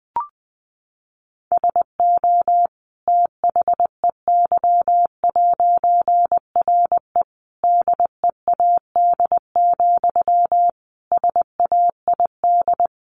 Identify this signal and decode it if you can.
Morse code, heard as a keyed tone, about 19 wpm, 700 Hz. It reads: SO THEY'RE DEAD, SAID
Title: A Sound Of Thunder 20wpm